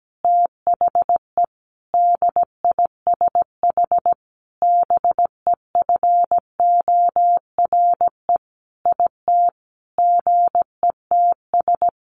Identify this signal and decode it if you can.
Morse code, heard as a keyed tone, about 17 wpm, 700 Hz. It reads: THE DISH BEFORE IT GETS
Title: Sentences from Top 1300 Words 17wpm